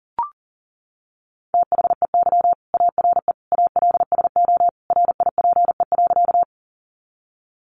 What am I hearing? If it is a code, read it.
Morse code, heard as a keyed tone, about 40 wpm, 700 Hz. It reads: THEY ARE ALSO RIPE.
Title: Additional Easy Phrases 40wpm